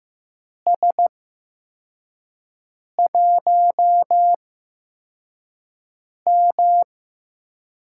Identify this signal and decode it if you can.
Morse code, heard as a keyed tone, about 15 wpm, spaced as for 4 wpm, 700 Hz. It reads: S1M
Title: USA Call Signs - 4 Characters - 8x Character Spacing - 15wpm